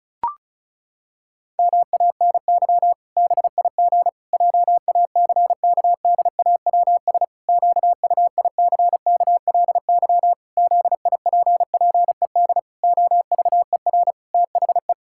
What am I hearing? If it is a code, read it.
Morse code, heard as a keyed tone, about 34 wpm, 700 Hz. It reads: MANY BIG JACKDAWS QUICKLY ZIPPED OVER THE
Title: Pangram Challenge v2 - 35wpm